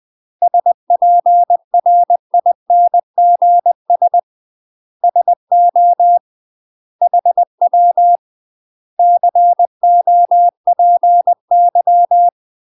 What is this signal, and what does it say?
Morse code, heard as a keyed tone, about 20 wpm, 700 Hz. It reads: SPRINGS SO HW COPY
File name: UK MASTER QSOs TRAINING / 20WPM / UK MASTER QSOs 0026